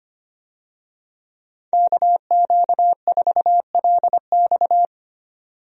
Morse code, heard as a keyed tone, about 25 wpm, 700 Hz. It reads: KQ4LX
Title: USA Call Signs - 5 Characters - 25wpm